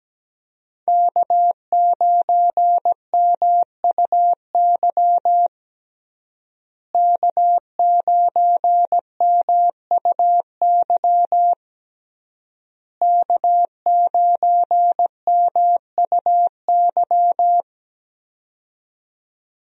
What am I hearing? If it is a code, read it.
Morse code, heard as a keyed tone, about 17 wpm, 700 Hz. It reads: K9MUY K9MUY K9MUY
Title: USA Call Signs - 5 Characters - Repeated 3x - 17wpm